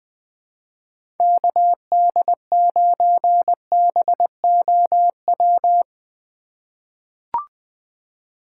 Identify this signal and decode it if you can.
Morse code, heard as a keyed tone, about 20 wpm, 700 Hz. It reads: KD9BOW
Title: USA Call Signs - 6 Characters - 2x Character Spacing - 20wpm